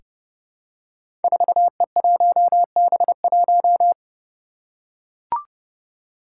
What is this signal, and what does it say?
Morse code, heard as a keyed tone, about 30 wpm, 700 Hz. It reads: 4E1B1
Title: Five Challenging Letter-Number Combinations 30wpm